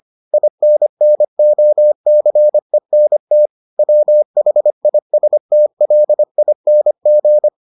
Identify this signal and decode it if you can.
Morse code, heard as a keyed tone, about 25 wpm, 600 Hz. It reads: INNOCENT WHISTLING